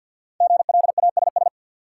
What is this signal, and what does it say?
Morse code, heard as a keyed tone, about 53 wpm, 700 Hz. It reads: GLASS
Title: Sentences from Top 1300 Words - New Word Drill Alt - 50wpm